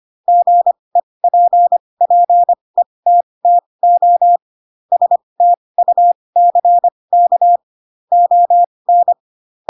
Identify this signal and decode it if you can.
Morse code, heard as a keyed tone, about 25 wpm, spaced as for 20 wpm, 700 Hz. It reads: GEPPETTO STUCK ON
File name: Pinocchio / Pinocchio ch8 15 25 ik1hge